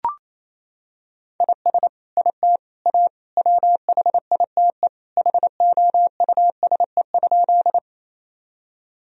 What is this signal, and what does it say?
Morse code, heard as a keyed tone, about 28 wpm, 700 Hz. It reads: IS IT A WHITE HOUSE?
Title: Sentences from Top 1000 Words - Review Part 1 - 28wpm